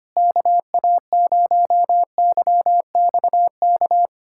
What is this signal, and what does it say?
Morse code, heard as a keyed tone, about 25 wpm, 700 Hz. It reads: KA0YXK
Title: USA Callsign DE Callsign - 6 Characters - 25wpm